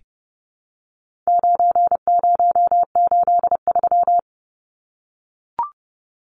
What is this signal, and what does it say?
Morse code, heard as a keyed tone, about 30 wpm, 700 Hz. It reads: 9083
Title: Numbers - 4-Digit 30wpm